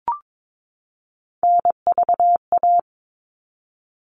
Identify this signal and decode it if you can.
Morse code, heard as a keyed tone, about 22 wpm, 700 Hz. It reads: NVA
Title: All Cut Numbers - 3-Digit 22wpm